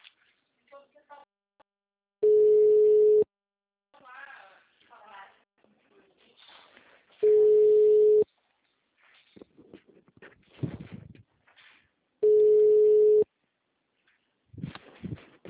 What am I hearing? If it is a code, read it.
Ringback tone (Europe, 425 Hz), with speech.
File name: out- -2025- - - .0